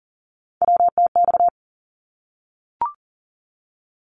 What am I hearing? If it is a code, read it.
Morse code, heard as a keyed tone, about 40 wpm, 700 Hz. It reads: WTX
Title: ARRL Sections 40wpm